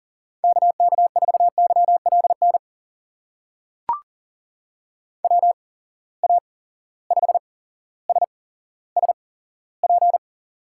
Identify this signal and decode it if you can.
Morse code, heard as a keyed tone, about 40 wpm, spaced as for 17 wpm, 700 Hz. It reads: KK4YLN WA5SSP
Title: USA Call Signs - 6 Characters - 8x Character Spacing - 40wpm